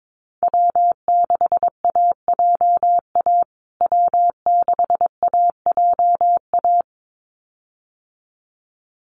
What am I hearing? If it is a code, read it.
Morse code, heard as a keyed tone, about 22 wpm, 700 Hz. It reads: W6AJA W6AJA
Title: USA Call Signs Repeated 2x 22wpm